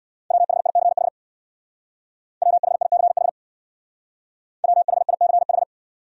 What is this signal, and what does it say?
Morse code, heard as a keyed tone, about 65 wpm, 700 Hz. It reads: WHICH WHICH WHICH